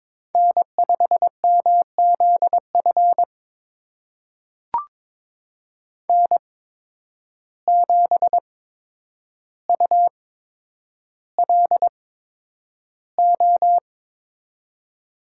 Morse code, heard as a keyed tone, about 22 wpm, spaced as for 9 wpm, 700 Hz. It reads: N5MZF N7ULO
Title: USA Call Signs - 5 Characters - 8x Character Spacing - 22wpm